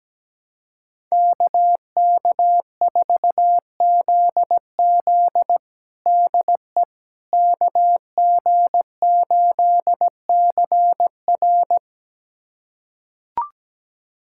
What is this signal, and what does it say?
Morse code, heard as a keyed tone, about 17 wpm, 700 Hz. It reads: KK4ZZ DE KG8CR
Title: USA Callsign DE Callsign - 5 Characters - 17wpm